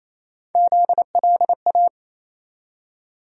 Morse code, heard as a keyed tone, about 28 wpm, 700 Hz. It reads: ZLA
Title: USA Call Sign Postfixes - 3 Letters 28wpm